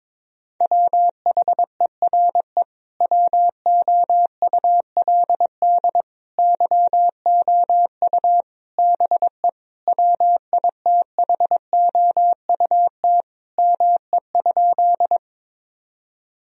Morse code, heard as a keyed tone, about 22 wpm, 700 Hz. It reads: WHERE WOULD YOU BE WITHOUT ME?
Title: Sentences from Top 1000 Words - Review Part 3 - 22wpm